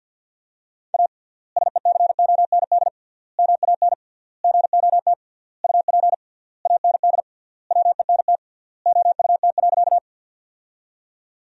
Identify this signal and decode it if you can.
Morse code, heard as a keyed tone, about 47 wpm, 700 Hz. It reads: A SECOND MAN GOT UP AND WENT OUT.
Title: 3x Word-Spacing - Sentences from Top 300 Words - 50wpm